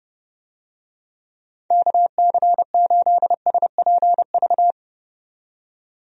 Morse code, heard as a keyed tone, about 30 wpm, 700 Hz. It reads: KC8SPV
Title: USA Call Signs - 6 Characters - 30wpm